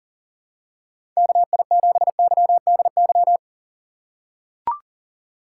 Morse code, heard as a keyed tone, about 40 wpm, 700 Hz. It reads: KI7YDY
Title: USA Call Signs - 6 Characters - 3x Character Spacing - 40wpm